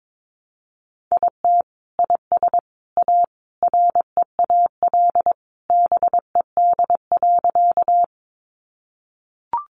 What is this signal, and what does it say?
Morse code, heard as a keyed tone, about 22 wpm, 700 Hz. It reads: IT IS A REAL BED.
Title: Sentences from Top 500 Words 22wpm